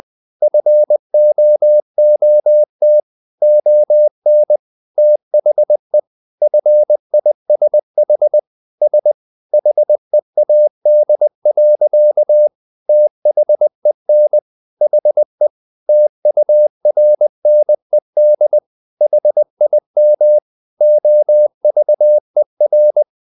Morse code, heard as a keyed tone, about 20 wpm, 600 Hz. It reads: FOOT ON THE FISH S HEAD. THEN HE TURNED HIM OVER
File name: the old man and the sea part 2 20wpm